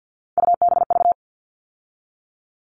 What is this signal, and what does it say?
Morse code, heard as a keyed tone, about 50 wpm, 700 Hz. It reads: UBV
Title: All Cut Numbers - 3-Digit 50wpm